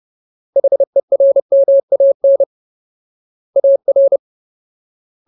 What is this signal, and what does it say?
Morse code, heard as a keyed tone, about 30 wpm, 550 Hz. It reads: HERMAN AR
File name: Ultimate Word List / 30WPM Ultimate